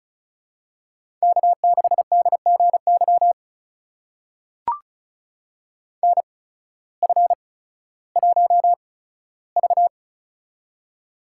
Morse code, heard as a keyed tone, about 36 wpm, spaced as for 16 wpm, 700 Hz. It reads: K6DGY NF1V
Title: USA Call Signs - 8x Character Spacing - 35wpm